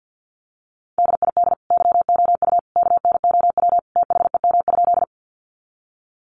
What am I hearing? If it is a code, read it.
Morse code, heard as a keyed tone, about 50 wpm, 700 Hz. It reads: DID YOU KNOW THEM?
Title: Sentences from Top 100 Words 50wpm